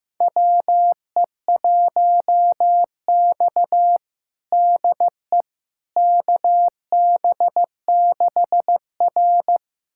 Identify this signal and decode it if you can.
Morse code, heard as a keyed tone, about 15 wpm, 700 Hz. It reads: WE1X DE KB6R
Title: USA Callsign DE Callsign - 4 Characters - 15wpm